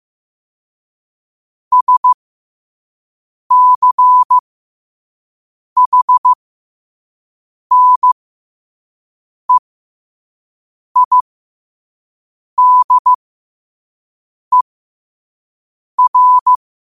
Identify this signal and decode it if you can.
Morse code, heard as a keyed tone, about 15 wpm, spaced as for 5 wpm, 1000 Hz. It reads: SCHNEIDER
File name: qotdfiles / 5wpm